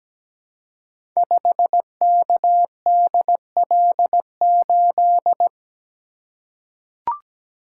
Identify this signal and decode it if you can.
Morse code, heard as a keyed tone, about 17 wpm, 700 Hz. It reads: 5KDL8
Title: Code Groups - 5 Letter:Numbers - 17wpm